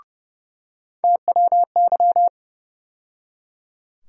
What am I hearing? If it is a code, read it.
Morse code, heard as a keyed tone, about 30 wpm, 700 Hz. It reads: TWY
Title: USA Call Sign Postfixes - 3 Letters 30wpm